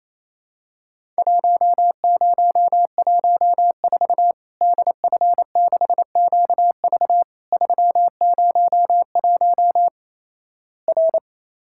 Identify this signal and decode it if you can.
Morse code, heard as a keyed tone, about 28 wpm, 700 Hz. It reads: 1014 DF6QV 301 R
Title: Worked All Europe DX Contest - QTC Exchange - 28wpm